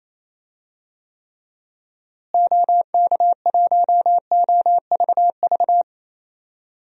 Morse code, heard as a keyed tone, about 28 wpm, 700 Hz. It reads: OK1OVV
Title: DXCC-100 Call Signs - 6 Characters - Repeated 2x - 28wpm